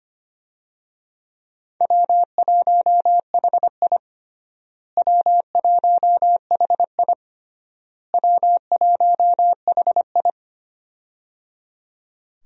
Morse code, heard as a keyed tone, about 25 wpm, 700 Hz. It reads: W1HI W1HI W1HI
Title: USA Call Signs - 4 Characters - Repeated 3x - 25wpm